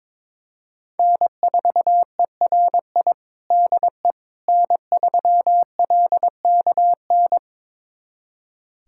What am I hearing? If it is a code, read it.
Morse code, heard as a keyed tone, about 22 wpm, 700 Hz. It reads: N4ERI DE N3LKN
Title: USA Callsign DE Callsign - 5 Characters - 22wpm